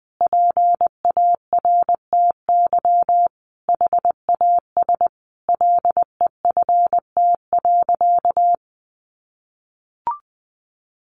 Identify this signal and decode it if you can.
Morse code, heard as a keyed tone, about 20 wpm, 700 Hz. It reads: PARTY HAS LEFT.
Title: Sentences from Top 1500 Words 20wpm